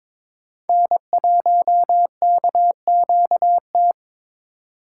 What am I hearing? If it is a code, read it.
Morse code, heard as a keyed tone, about 22 wpm, 700 Hz. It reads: N1KQT
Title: USA Call Signs - 5 Characters - 8x Character Spacing - 22wpm